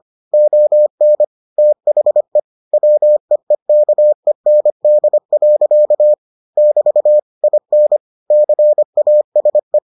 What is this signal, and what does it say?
Morse code, heard as a keyed tone, about 25 wpm, 600 Hz. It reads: ON THE WEEKEND. <BT> IN CASE